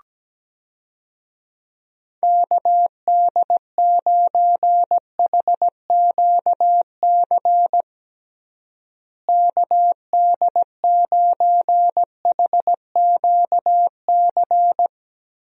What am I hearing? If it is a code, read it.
Morse code, heard as a keyed tone, about 17 wpm, 700 Hz. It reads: KD9HQC KD9HQC
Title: USA Call Signs - 6 Characters - Repeated 3x - 17wpm